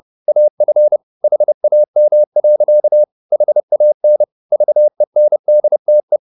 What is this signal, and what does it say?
Morse code, heard as a keyed tone, about 30 wpm, 600 Hz. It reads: AF HAM. HAN VENDTE
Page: Ferie 30 wpm